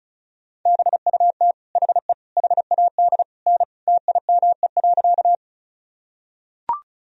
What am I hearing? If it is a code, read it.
Morse code, heard as a keyed tone, about 34 wpm, 700 Hz. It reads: BUT HE HAD N TIME.
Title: Sentences from Top 100 Words and Abbreviations 35wpm